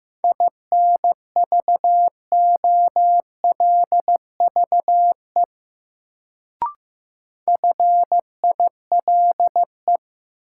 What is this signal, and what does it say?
Morse code, heard as a keyed tone, about 15 wpm, 700 Hz. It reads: INVOLVE FILE
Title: Sentences from Top 1200 Words - New Word Drill Alt - 15wpm